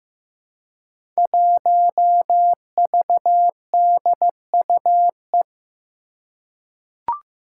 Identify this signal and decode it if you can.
Morse code, heard as a keyed tone, about 15 wpm, 700 Hz. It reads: 1VDUE
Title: Five Challenging Letter-Number Combinations 15wpm